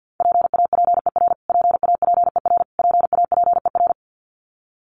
Morse code, heard as a keyed tone, about 49 wpm, 700 Hz. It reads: PAPER PAPER PAPER
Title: Top 500 Words Repeated 50wpm